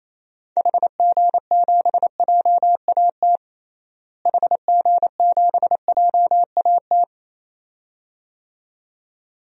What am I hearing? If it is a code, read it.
Morse code, heard as a keyed tone, about 27 wpm, 700 Hz. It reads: HG7JAT HG7JAT
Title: DXCC-100 Call Signs - 6 Characters - Repeated 2x - 28wpm